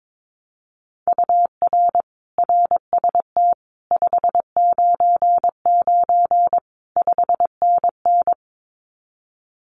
Morse code, heard as a keyed tone, about 22 wpm, 700 Hz. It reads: UR RST 599 5NN
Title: Common QSO Phrases Encore 22wpm